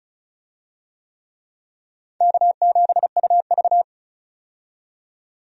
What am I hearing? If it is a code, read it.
Morse code, heard as a keyed tone, about 34 wpm, 700 Hz. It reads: K7UV
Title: USA Call Signs - 4 Characters - 35wpm